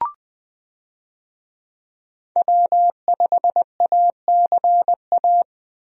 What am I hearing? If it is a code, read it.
Morse code, heard as a keyed tone, about 20 wpm, 700 Hz. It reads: W5ACA
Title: USA Call Signs - 5 Characters - 20wpm